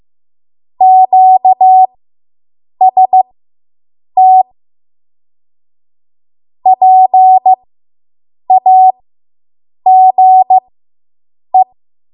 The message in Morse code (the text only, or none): QST PAGE